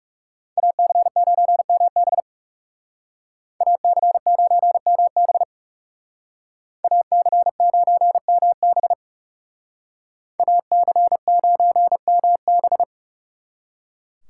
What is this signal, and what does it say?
Morse code, heard as a keyed tone, about 38 wpm, 700 Hz. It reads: AC9MB AC9MB AC9MB AC9MB